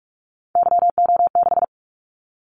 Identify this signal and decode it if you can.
Morse code, heard as a keyed tone, about 44 wpm, 700 Hz. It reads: YO6
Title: Callsign Prefixes for Top 25 DXCC - 45wpm